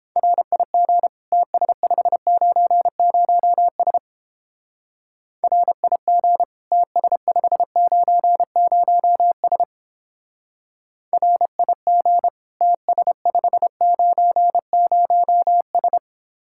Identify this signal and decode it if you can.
Morse code, heard as a keyed tone, about 29 wpm, 700 Hz. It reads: RIG TS590S RIG TS590S RIG TS590S